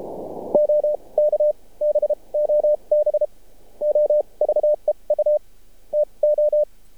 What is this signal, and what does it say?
Morse code, heard as a keyed tone, about 30 wpm, spaced as for 24 wpm, 600 Hz. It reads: OK BOB OVEU TO
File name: archived / CRSnet